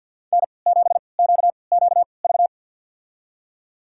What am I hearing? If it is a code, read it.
Morse code, heard as a keyed tone, about 50 wpm, spaced as for 28 wpm, 700 Hz. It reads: N7QQV
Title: USA Call Signs - 5 Characters - 3x Character Spacing - 50wpm